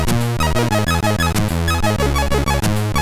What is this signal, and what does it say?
There is no code.